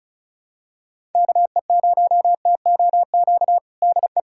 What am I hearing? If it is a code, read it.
Morse code, heard as a keyed tone, about 36 wpm, 700 Hz. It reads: KE0TOQ DE
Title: USA Callsign DE Callsign - 6 Characters - 35wpm